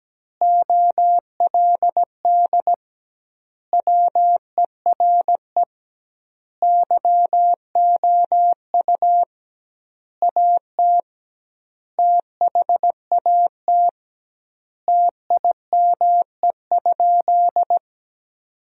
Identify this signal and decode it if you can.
Morse code, heard as a keyed tone, about 17 wpm, 700 Hz. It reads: OLD WERE YOU AT THAT TIME?
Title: 2x Word-Spacing - Sentences from Top 200 Words - 17wpm